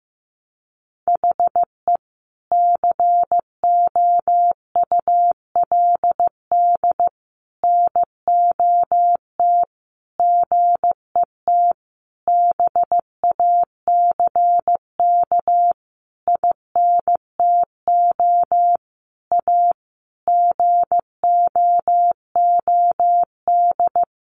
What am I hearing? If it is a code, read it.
Morse code, heard as a keyed tone, about 15 wpm, 700 Hz. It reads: HE COULD NOT GET BACK INTO A GOOD
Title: Sentences from Top 1100 Words 15wpm